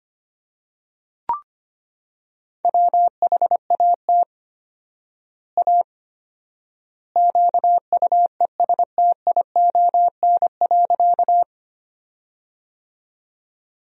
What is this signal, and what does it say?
Morse code, heard as a keyed tone, about 25 wpm, 700 Hz. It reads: WHAT A QUESTION.
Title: 4x Word-Spacing - Sentences from Top 400 Words - 25wpm